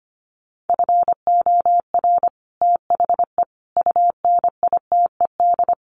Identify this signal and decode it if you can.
Morse code, heard as a keyed tone, about 25 wpm, 700 Hz. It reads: FOR THE UNITED
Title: Sets of 4 Words 25wpm